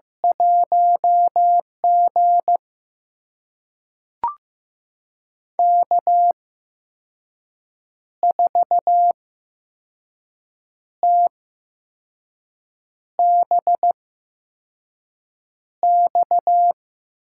Morse code, heard as a keyed tone, about 15 wpm, spaced as for 5 wpm, 700 Hz. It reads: 1G K4TBX
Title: USA Call Signs - 8x Character Spacing - 15wpm